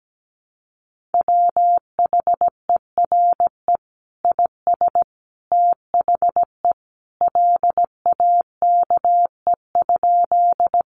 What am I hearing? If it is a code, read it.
Morse code, heard as a keyed tone, about 17 wpm, 700 Hz. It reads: WHERE IS THE LAKE?
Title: Sentences from Top 1100 Words 17wpm